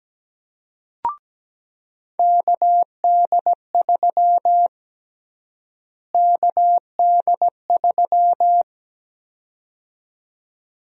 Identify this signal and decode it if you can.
Morse code, heard as a keyed tone, about 17 wpm, 700 Hz. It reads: KD3 KD3
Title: USA Call Sign Prefixes - Repeated 2x - 17wpm